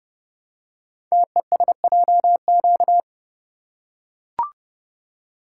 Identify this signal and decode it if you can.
Morse code, heard as a keyed tone, about 30 wpm, 700 Hz. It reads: TESJQ
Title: Code Groups - 5 Letters - 30wpm